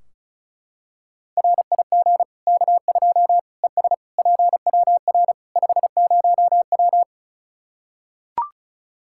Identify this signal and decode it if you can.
Morse code, heard as a keyed tone, about 34 wpm, 700 Hz. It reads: RIG K2 ES PWR 50W
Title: Practice QSO - Exchange 2 - Info - 35wpm